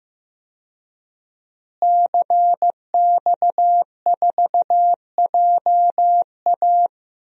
Morse code, heard as a keyed tone, about 15 wpm, 700 Hz. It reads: CX4JA
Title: DXCC-100 Call Signs - 5 Characters - Repeated 2x - 15wpm